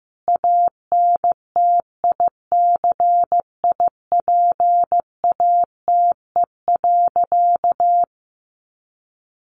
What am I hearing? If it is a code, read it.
Morse code, heard as a keyed tone, about 15 wpm, 700 Hz. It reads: ANTICIPATE.
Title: Sentences from Top 1500 Words 15wpm